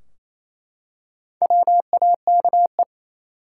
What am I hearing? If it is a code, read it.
Morse code, heard as a keyed tone, about 28 wpm, 700 Hz. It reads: WAKE
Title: Four Letter Words - 28wpm